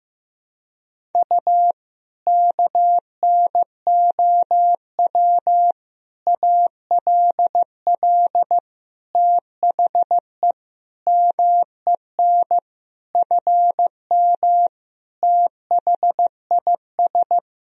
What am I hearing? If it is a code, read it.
Morse code, heard as a keyed tone, about 15 wpm, 700 Hz. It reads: U KNOW ALL THE MEN FM THIS
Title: Sentences from Top 300 Words and Abbreviations 15wpm